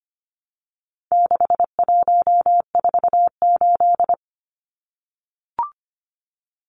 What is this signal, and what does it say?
Morse code, heard as a keyed tone, about 25 wpm, 700 Hz. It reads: 6148
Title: Numbers - 4-Digit 25wpm